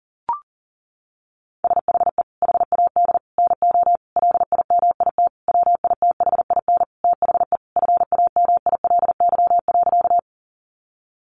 Morse code, heard as a keyed tone, about 40 wpm, 700 Hz. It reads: SHE HAD NO LIMIT WITHIN THE FAMILY.
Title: Sentences from Top 1500 Words 40wpm